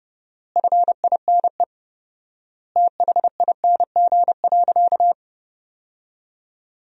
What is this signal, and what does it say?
Morse code, heard as a keyed tone, about 30 wpm, 700 Hz. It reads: FINE THING.
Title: 4x Word-Spacing - Sentences from Top 500 Words - 30wpm